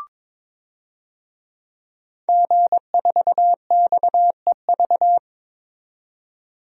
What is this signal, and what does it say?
Morse code, heard as a keyed tone, about 22 wpm, 700 Hz. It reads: G4XEV
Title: DXCC-100 Call Signs - 5 Characters - 22wpm